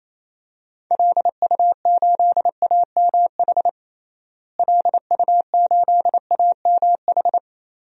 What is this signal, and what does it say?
Morse code, heard as a keyed tone, about 28 wpm, 700 Hz. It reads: LU8AMH LU8AMH
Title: DXCC-100 Call Signs - 6 Characters - Repeated 2x - 28wpm